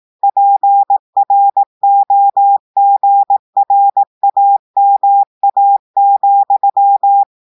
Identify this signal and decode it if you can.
Morse code, heard as a keyed tone, about 18 wpm, 800 Hz. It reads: PROGRAMA,
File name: audios / portugais / portugais texte en clair 2 vitesse 18 mots minutes